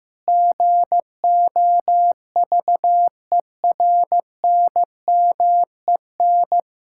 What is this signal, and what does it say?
Morse code, heard as a keyed tone, about 16 wpm, 700 Hz. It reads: GOVERNMEN
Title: Sentences from Top 1000 Words - Review Part 5 - 15wpm